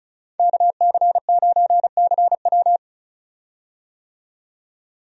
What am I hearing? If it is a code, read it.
Morse code, heard as a keyed tone, about 36 wpm, 700 Hz. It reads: KC9CW
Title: USA Call Signs - 5 Characters - 35wpm